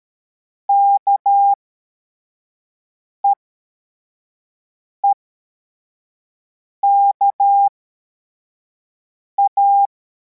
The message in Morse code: KEEKA